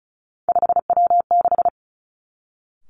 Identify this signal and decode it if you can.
Morse code, heard as a keyed tone, about 36 wpm, 700 Hz. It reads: 5W6
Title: Three Challenging Letter-Number Combinations 35wpm